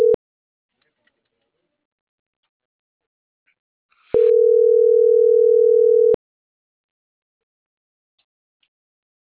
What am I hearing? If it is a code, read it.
Ringback tone (North America), 440 + 480 Hz.